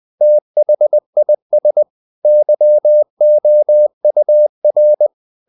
Morse code, heard as a keyed tone, about 20 wpm, 600 Hz. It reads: THIS YOUR
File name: the old man and the sea / the old man and the sea part 1 20wpm